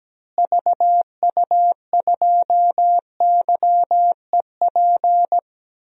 Morse code, heard as a keyed tone, about 17 wpm, 700 Hz. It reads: VU2YEP
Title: DXCC-100 Call Signs - 6 Characters - Repeated 2x - 17wpm